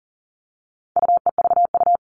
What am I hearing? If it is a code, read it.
Morse code, heard as a keyed tone, about 40 wpm, 700 Hz. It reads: UEVU
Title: All Cut Numbers - 4-Digit 40wpm